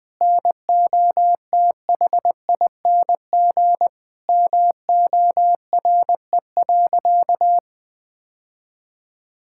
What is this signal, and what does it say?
Morse code, heard as a keyed tone, about 21 wpm, 700 Hz. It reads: NOTHING MORE.
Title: Sentences from Top 1000 Words - Review Part 5 - 20wpm